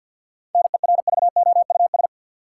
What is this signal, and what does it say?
Morse code, heard as a keyed tone, about 52 wpm, 700 Hz. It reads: NERVOUS
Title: Seven Letter Words - 50wpm